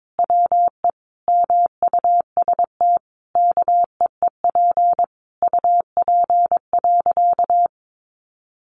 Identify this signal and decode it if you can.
Morse code, heard as a keyed tone, about 22 wpm, 700 Hz. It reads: WE MUST KEEP UP.
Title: Sentences from Top 700 Words 22wpm